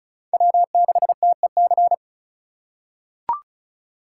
Morse code, heard as a keyed tone, about 34 wpm, 700 Hz. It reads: W6TEC
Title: USA Call Signs - 5 Characters - 4x Character Spacing - 35wpm